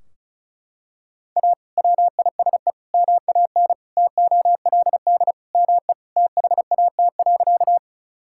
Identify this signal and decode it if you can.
Morse code, heard as a keyed tone, about 34 wpm, 700 Hz. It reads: A WISE MAN TOLD ME THAT.
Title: Sentences from Top 1600 Words 35wpm